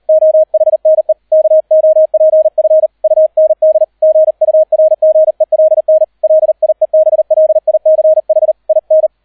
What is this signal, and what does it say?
Morse code, heard as a keyed tone, about 38 wpm, 630 Hz, with radio noise. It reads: OHNE KOPF UND GURGELN LIEBLICH IN